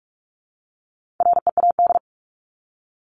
Morse code, heard as a keyed tone, about 44 wpm, 700 Hz. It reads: READ
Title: ING Suffix - Top 100 Words with Base Words 45wpm